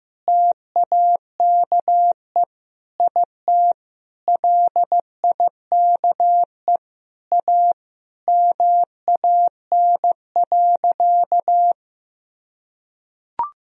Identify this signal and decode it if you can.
Morse code, heard as a keyed tone, about 16 wpm, 700 Hz. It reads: TAKE IT LIKE A MAN.
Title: Sentences from Top 300 Words and Abbreviations 15wpm